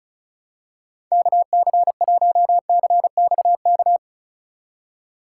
Morse code, heard as a keyed tone, about 36 wpm, 700 Hz. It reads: KC1CXK
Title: USA Call Signs - 4x Character Spacing - 35wpm